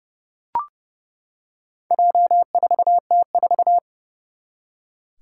Morse code, heard as a keyed tone, about 30 wpm, 700 Hz. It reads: J4T4
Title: Four Challenging Letter-Number Combinations 30wpm